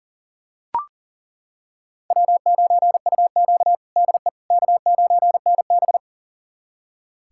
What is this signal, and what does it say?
Morse code, heard as a keyed tone, about 40 wpm, 700 Hz. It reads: W9UQ DE K9NB
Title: USA Callsign DE Callsign - 4 Characters - 40wpm